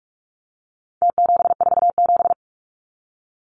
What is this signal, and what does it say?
Morse code, heard as a keyed tone, about 46 wpm, 700 Hz. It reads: T747
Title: Numbers - Contest Style - 4-Digit 45wpm